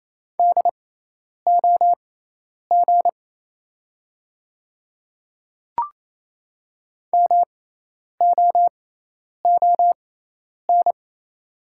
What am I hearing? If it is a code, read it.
Morse code, heard as a keyed tone, about 28 wpm, spaced as for 10 wpm, 700 Hz. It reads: DOG MOON
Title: 6x Character Spacing - Top 500 Words - 28wpm